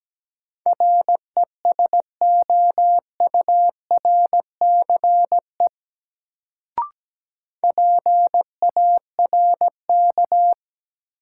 Morse code, heard as a keyed tone, about 17 wpm, 700 Hz. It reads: RESOURCE PARK
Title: Sentences from Top 1300 Words - New Word Drill - 17wpm